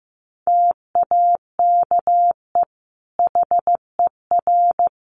Morse code, heard as a keyed tone, about 15 wpm, 700 Hz. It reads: TAKE HER
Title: Sentences from Top 1100 Words 15wpm